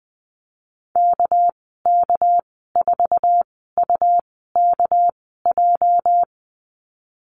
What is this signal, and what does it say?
Morse code, heard as a keyed tone, about 20 wpm, spaced as for 14 wpm, 700 Hz. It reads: KK4UKJ
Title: 245 Lesson 39g - Letter z - Call Signs 20wpm